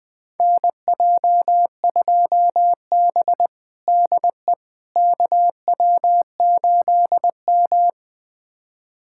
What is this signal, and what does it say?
Morse code, heard as a keyed tone, about 20 wpm, 700 Hz. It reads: NJ2B DE KW8M
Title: USA Callsign DE Callsign - 4 Characters - 20wpm